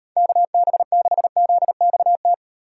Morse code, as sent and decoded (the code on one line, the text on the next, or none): -.- -... -.... --.. -..- -
KB6ZXT